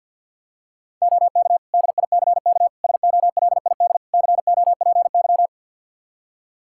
Morse code, heard as a keyed tone, about 51 wpm, 700 Hz. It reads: OK DICK SOLID COPY
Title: Practice QSO - Exchange 3 - More Info - 50wpm